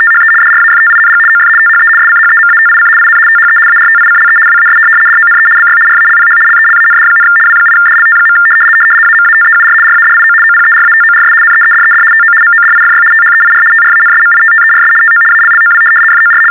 RTTY, about 75 baud. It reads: THE DRAFTING OF THE SCHEMATIC IN THE ARTICLE WAS DONE BY RALPH LELAND, W8DLT. AMATEUR RADIO OPERATORS NEEDED TO MODIFY THEIR TRANSMITTERS TO ALLOW FOR
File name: teletype-RTTY-75